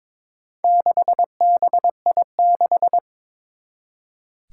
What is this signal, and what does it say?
Morse code, heard as a keyed tone, about 22 wpm, 700 Hz. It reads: 6BI6
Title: Four Challenging Letter-Number Combinations 22wpm